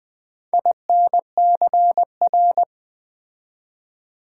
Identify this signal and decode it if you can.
Morse code, heard as a keyed tone, about 20 wpm, 700 Hz. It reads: INCR
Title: Build A Word - 4 Letters At A Time - 12-Letter Words - 20wpm